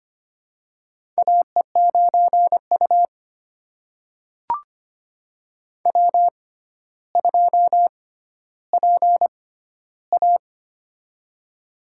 Morse code, heard as a keyed tone, about 25 wpm, spaced as for 13 wpm, 700 Hz. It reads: AE9U W2PA
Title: USA Call Signs - 4 Characters - 6x Character Spacing - 25wpm